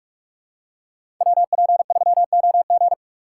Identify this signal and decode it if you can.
Morse code, heard as a keyed tone, about 44 wpm, 700 Hz. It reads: WP3OG
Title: USA Call Signs - 5 Characters - 3x Character Spacing - 45wpm